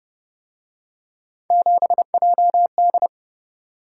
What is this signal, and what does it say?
Morse code, heard as a keyed tone, about 29 wpm, 700 Hz. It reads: 7JD
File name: Speed-Racing - Three Letter-Number Combinations - 25wpm